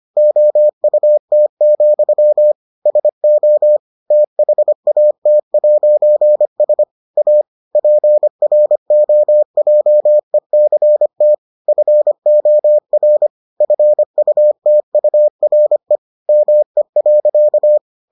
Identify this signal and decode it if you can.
Morse code, heard as a keyed tone, about 25 wpm, 600 Hz. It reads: OUT, SO THAT'S A PROJECT FOR FUTURE ME.